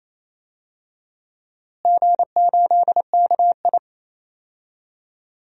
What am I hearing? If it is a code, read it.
Morse code, heard as a keyed tone, about 28 wpm, 700 Hz. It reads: G8KI
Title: DXCC-100 Call Signs - 4 Characters - Repeated 2x - 28wpm